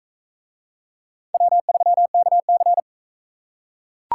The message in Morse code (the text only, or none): W3KC